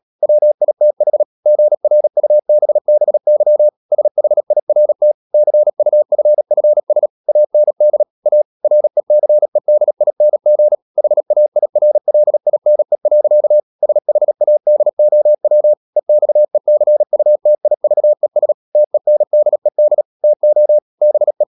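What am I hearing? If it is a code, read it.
Morse code, heard as a keyed tone, about 37 wpm, 600 Hz. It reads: WITH GRUBBY SHIRT CUFFS AND A RECEDING HAIRLINE. SHADOW EXECUTIVES TENDED TO BE
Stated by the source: Morse code practice at 35–39 wpm sending English text of Ken Follett's novel Never, from CW med Gnister.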